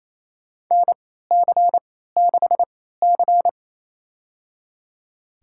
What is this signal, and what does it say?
Morse code, heard as a keyed tone, about 27 wpm, spaced as for 16 wpm, 700 Hz. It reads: NC6C
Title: USA Call Signs - 4 Characters - 3x Character Spacing - 28wpm